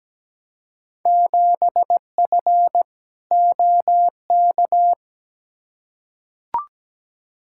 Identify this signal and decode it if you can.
Morse code, heard as a keyed tone, about 17 wpm, 700 Hz. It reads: 7F OK
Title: Field Day Exchanges Encore - 17wpm